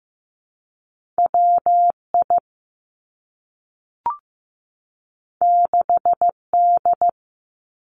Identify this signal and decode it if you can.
Morse code, heard as a keyed tone, about 15 wpm, 700 Hz. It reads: WI 6D
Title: Two Challenging Letter-Number Combinations 15wpm